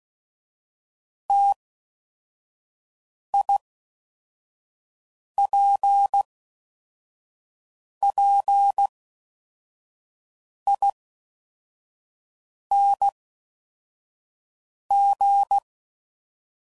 Morse code, heard as a keyed tone, about 16 wpm, 800 Hz. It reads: T I P P I N G
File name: Combinations / Difficult Combos / T AOEI's